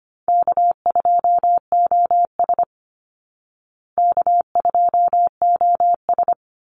Morse code, heard as a keyed tone, about 25 wpm, 700 Hz. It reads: K2OS K2OS
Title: Call Signs Repeated 3x - 25wpm